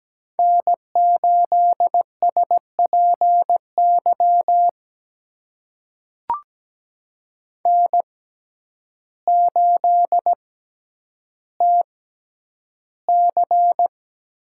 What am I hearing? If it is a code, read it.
Morse code, heard as a keyed tone, about 17 wpm, spaced as for 9 wpm, 700 Hz. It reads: N8SPY N8TC
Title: USA Call Signs - 5 Characters - 6x Character Spacing - 17wpm